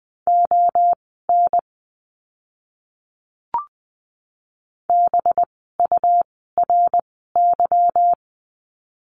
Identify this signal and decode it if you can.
Morse code, heard as a keyed tone, about 20 wpm, spaced as for 14 wpm, 700 Hz. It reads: ON BURY
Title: 185 Lesson 30f - Letter b - Words 20wpm